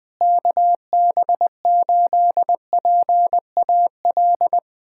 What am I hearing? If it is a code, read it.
Morse code, heard as a keyed tone, about 20 wpm, 700 Hz. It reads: KB8PAL
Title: USA Callsign DE Callsign 20wpm